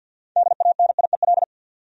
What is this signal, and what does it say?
Morse code, heard as a keyed tone, about 51 wpm, 700 Hz. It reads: DANIEL
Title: USA Common Names - 50wpm